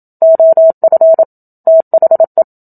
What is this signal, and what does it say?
Morse code, heard as a keyed tone, about 28 wpm, 650 Hz. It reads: OF THE
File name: Book Two / 01-Chapter 1 - 27 WPM